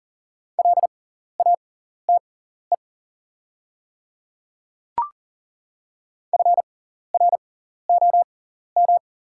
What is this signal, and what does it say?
Morse code, heard as a keyed tone, about 40 wpm, spaced as for 14 wpm, 690 Hz. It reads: LATE FROM
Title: 6x Character Spacing - Top 500 Words - 40wpm